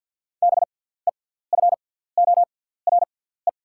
Morse code, heard as a keyed tone, about 48 wpm, spaced as for 17 wpm, 700 Hz. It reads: BEFORE
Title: 6x Character Spacing - Top 300 Words - 50wpm